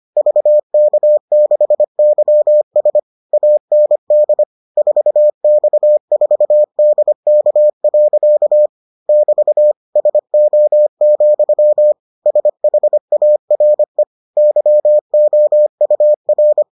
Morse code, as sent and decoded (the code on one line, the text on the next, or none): ...- -.- -.... -.-- ... / .- -. -.. / ....- -..- ....- -.. -.- .-.-.- / -...- / ... --- --..-- / ... .... .- .-. / -.-- --- ..- .-.
VK6YS AND 4X4DK. <BT> SO, SHARE YOUR